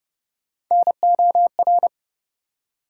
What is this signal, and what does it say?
Morse code, heard as a keyed tone, about 31 wpm, 700 Hz. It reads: NOR
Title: Three Letter Words - 30wpm